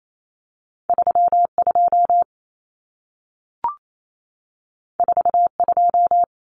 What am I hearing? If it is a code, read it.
Morse code, heard as a keyed tone, about 28 wpm, 700 Hz. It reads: 32 42
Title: Common Cut Numbers - 2-Digit 28wpm